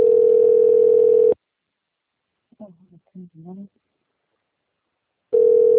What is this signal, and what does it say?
Ringback tone (North America), 440 + 480 Hz, with speech.
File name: rg-1600-unknown- - - .2853